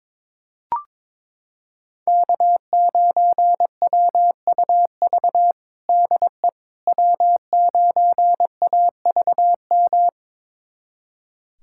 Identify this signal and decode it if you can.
Morse code, heard as a keyed tone, about 22 wpm, 700 Hz. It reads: K9WUV DE W9AVM
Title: USA Callsign DE Callsign - 5 Characters - 22wpm